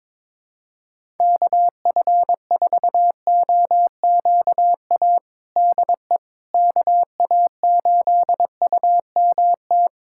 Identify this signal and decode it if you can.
Morse code, heard as a keyed tone, about 22 wpm, 700 Hz. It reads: KF4OQA DE KA8UMT
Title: USA Callsign DE Callsign - 6 Characters - 22wpm